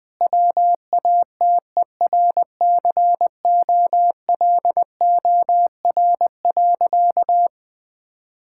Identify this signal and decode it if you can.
Morse code, heard as a keyed tone, about 20 wpm, 700 Hz. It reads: WATERCOLOR.
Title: Difficult Sentences with Hints - 20wpm